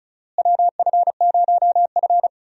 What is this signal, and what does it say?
Morse code, heard as a keyed tone, about 36 wpm, 700 Hz. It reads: WF0F
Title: USA Call Signs - 4 Characters - Repeated 2x - 35wpm